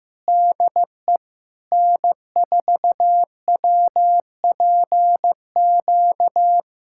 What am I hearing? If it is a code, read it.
Morse code, heard as a keyed tone, about 15 wpm, 700 Hz. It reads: DE N4WPQ
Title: SOTA QSO Practice 15wpm